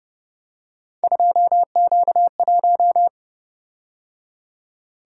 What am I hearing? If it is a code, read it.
Morse code, heard as a keyed tone, about 29 wpm, 700 Hz. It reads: 2Q1
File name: Speed-Racing - Three Letter-Number Combinations - 25wpm